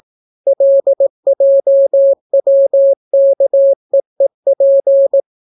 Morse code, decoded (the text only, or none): LJWKEEP